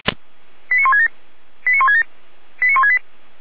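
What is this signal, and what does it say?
Five-tone selective call (ZVEI), heard as quick runs of five tones, tones of about 70 ms. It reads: 89167 89167 89168
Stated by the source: Received on an SDR, ZVEI five-tone selective call.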